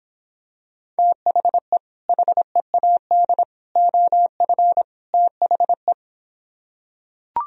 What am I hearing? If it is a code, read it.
Morse code, heard as a keyed tone, about 26 wpm, 700 Hz. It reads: THE HEAD OF THE